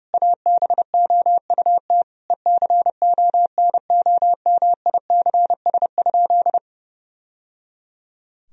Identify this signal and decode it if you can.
Morse code, heard as a keyed tone, about 32 wpm, 700 Hz. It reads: ABOUT ECONOMICS?
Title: Sentences from Top 1000 Words - Review Part 3 - 30wpm